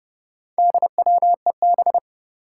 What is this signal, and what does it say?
Morse code, heard as a keyed tone, about 30 wpm, 700 Hz. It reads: DWEB
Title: Four Challenging Letter-Number Combinations 30wpm